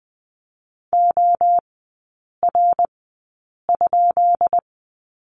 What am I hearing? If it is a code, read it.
Morse code, heard as a keyed tone, about 20 wpm, spaced as for 8 wpm, 700 Hz. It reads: OR?
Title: 133 Lesson 22e - Question Mark - Three Characters 20wpm